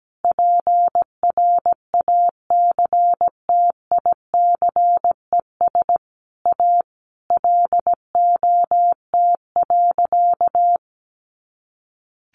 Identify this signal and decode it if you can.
Morse code, heard as a keyed tone, about 17 wpm, 700 Hz. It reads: PRACTICES A LOT.
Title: Additional Easy Phrases 17wpm